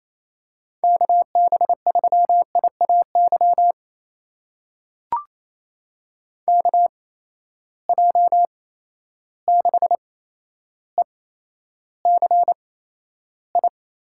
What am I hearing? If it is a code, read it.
Morse code, heard as a keyed tone, about 28 wpm, spaced as for 12 wpm, 700 Hz. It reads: KB3IAY KJ6ECI
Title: USA Call Signs - 6 Characters - 8x Character Spacing - 28wpm